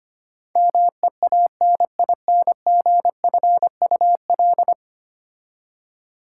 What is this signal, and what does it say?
Morse code, heard as a keyed tone, about 25 wpm, 700 Hz. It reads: MEANINGFUL
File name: 10 Letter Words - First 3 Letters Given - 25wpm